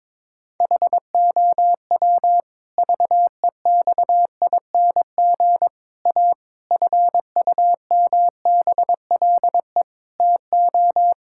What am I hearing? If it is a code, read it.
Morse code, heard as a keyed tone, about 21 wpm, 700 Hz. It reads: HOW VEXING A FUMBLE TO
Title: Pangram Challenge v2 - 22wpm